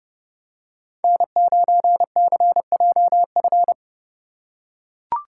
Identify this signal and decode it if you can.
Morse code, heard as a keyed tone, about 30 wpm, 700 Hz. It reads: N9CJF
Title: USA Call Signs - 5 Characters - 8x Character Spacing - 30wpm